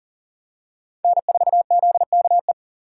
Morse code, heard as a keyed tone, about 39 wpm, 700 Hz. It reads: N4ZKE
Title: USA Call Signs - 5 Characters - 4x Character Spacing - 40wpm